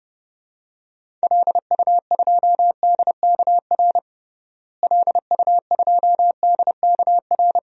Morse code, heard as a keyed tone, about 32 wpm, 700 Hz. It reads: LU2DKR LU2DKR
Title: DXCC-100 Call Signs - 6 Characters - Repeated 2x - 30wpm